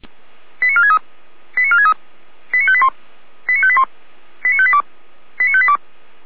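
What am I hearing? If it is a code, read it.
Five-tone selective call (ZVEI), heard as quick runs of five tones, tones of about 70 ms. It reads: 89463 89463 89581 89581 89582 89582
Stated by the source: Five-tone ZVEI selective call 89518, received on SDR.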